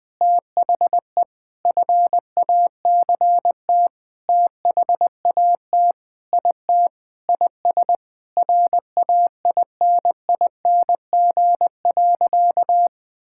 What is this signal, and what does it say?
Morse code, heard as a keyed tone, about 20 wpm, 700 Hz. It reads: THE FACT THAT IT IS RAINING.
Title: Sentences with 8-Word Phrases - 20wpm